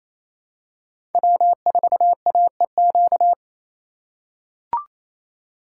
Morse code, heard as a keyed tone, about 27 wpm, 700 Hz. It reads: W4AEQ
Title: USA Call Signs - 5 Characters - 4x Character Spacing - 28wpm